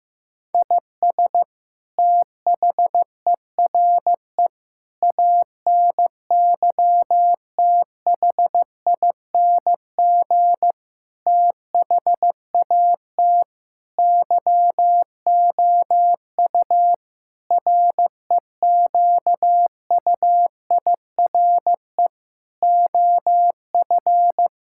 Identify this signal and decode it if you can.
Morse code, heard as a keyed tone, about 15 wpm, 700 Hz. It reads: IS THERE ANYTHING THAT YOU REQUIRE OF
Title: Sentences from Top 1000 Words - Review Part 6 - 15wpm